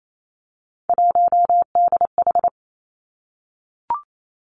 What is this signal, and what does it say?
Morse code, heard as a keyed tone, about 28 wpm, 700 Hz. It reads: 1DH
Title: Three Challenging Letter-Number Combinations 28wpm